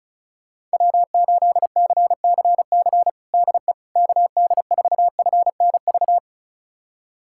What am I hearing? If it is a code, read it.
Morse code, heard as a keyed tone, about 34 wpm, 700 Hz. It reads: W8CCC DE KD4FNV
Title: USA Callsign DE Callsign 35wpm